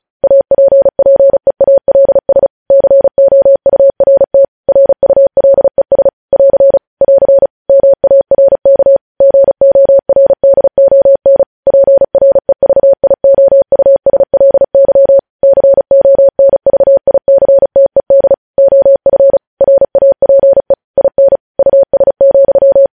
Morse code, heard as a keyed tone, about 36 wpm, 550 Hz. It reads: APPEALS COURT RULES <AR> <AR> MARK GORDON PREVIOUSLY CONVICTED OF RAPE IN US,
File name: podcast / News headlines-1x-35x35